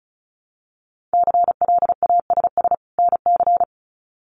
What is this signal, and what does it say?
Morse code, heard as a keyed tone, about 34 wpm, 700 Hz. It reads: CLASS NC
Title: Winter Field Day Exchanges 35wpm